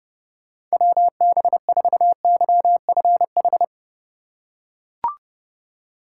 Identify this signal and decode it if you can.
Morse code, heard as a keyed tone, about 30 wpm, 700 Hz. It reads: WB4YFH
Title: USA Call Signs - 6 Characters - 2x Character Spacing - 30wpm